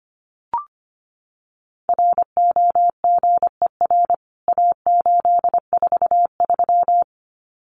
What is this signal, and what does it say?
Morse code, heard as a keyed tone, about 25 wpm, 700 Hz. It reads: ROGER A843
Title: CWT Exchange 25wpm